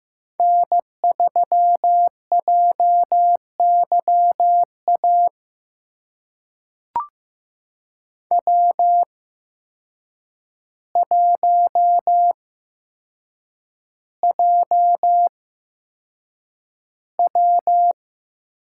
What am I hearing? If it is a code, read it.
Morse code, heard as a keyed tone, about 15 wpm, spaced as for 7 wpm, 700 Hz. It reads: N3JYA W1JW
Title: USA Call Signs - 5 Characters - 8x Character Spacing - 15wpm